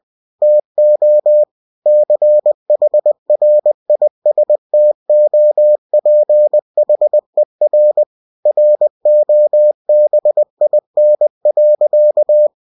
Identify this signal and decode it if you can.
Morse code, heard as a keyed tone, about 20 wpm, 600 Hz. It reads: TO CHRISTOPHER ROBIN.
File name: cw-books / winnie-the-pooh / winnie-20wpm-page-0100